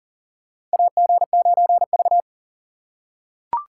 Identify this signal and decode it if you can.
Morse code, heard as a keyed tone, about 40 wpm, 700 Hz. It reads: AG9V
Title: USA Call Signs - 4 Characters - 3x Character Spacing - 40wpm